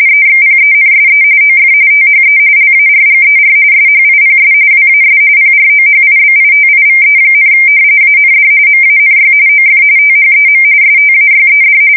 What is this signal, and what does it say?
RTTY, about 45.45 baud. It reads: EXPENSE OF THE LARGER SHIPS WAS PROHIBITIVE, AND MANY OF THEM STAYED
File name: fsk170 / usnavy2